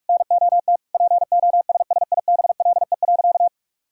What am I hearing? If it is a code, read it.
Morse code, heard as a keyed tone, about 45 wpm, 700 Hz. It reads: NOT POSSIBLE.
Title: Sentences from Top 1000 Words - Review Part 6 - 45wpm